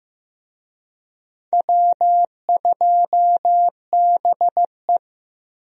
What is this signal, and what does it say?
Morse code, heard as a keyed tone, about 15 wpm, 700 Hz. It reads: W2BE
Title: USA Call Signs - 4 Characters - Repeated 2x - 15wpm